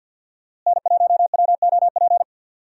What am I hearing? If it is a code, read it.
Morse code, heard as a keyed tone, about 50 wpm, 700 Hz. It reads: N1WOP
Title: USA Call Signs - 5 Characters - Repeated 3x - 50wpm